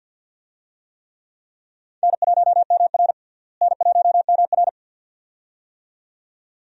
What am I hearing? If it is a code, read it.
Morse code, heard as a keyed tone, about 52 wpm, 700 Hz. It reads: N1MR N1MR
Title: USA Call Signs - 4 Characters - Repeated 2x - 50wpm